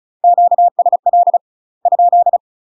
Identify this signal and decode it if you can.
Morse code, heard as a keyed tone, about 34 wpm, 700 Hz. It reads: QSL ?
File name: UK MASTER QSOs TRAINING / 35WPM / UK MASTER QSOs 0037